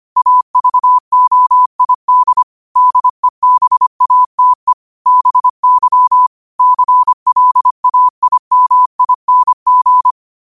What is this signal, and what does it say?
Morse code, heard as a keyed tone, about 25 wpm, 1000 Hz. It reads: AVOID DEBATE BY CLAIMING
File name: qotdfiles / 25wpm